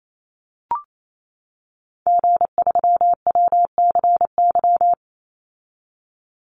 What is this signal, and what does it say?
Morse code, heard as a keyed tone, about 28 wpm, 700 Hz. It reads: G3WCY
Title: Call Signs - 28wpm